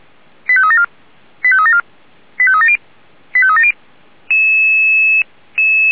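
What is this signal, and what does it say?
Five-tone selective call (ZVEI), heard as quick runs of five tones, tones of about 70 ms. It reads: 85374 85374 85370 85370
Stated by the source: Five-tone ZVEI selective call received by SDR.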